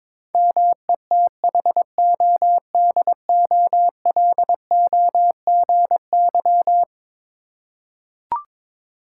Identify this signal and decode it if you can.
Morse code, heard as a keyed tone, about 22 wpm, 700 Hz. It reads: METHODOLOGY
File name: practice / 22wpm / Eleven Letter Words - 22wpm